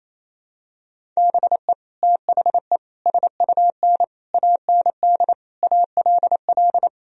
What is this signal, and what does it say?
Morse code, heard as a keyed tone, about 28 wpm, 700 Hz. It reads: BE THE SUN AND ALL
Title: Sentences from Top 1000 Words - Review Part 1 - 28wpm